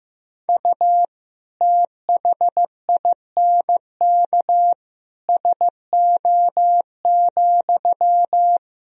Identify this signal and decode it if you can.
Morse code, heard as a keyed tone, about 15 wpm, 700 Hz. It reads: U THINK SO,
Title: Sentences from Top 200 Words and Abbreviations 15wpm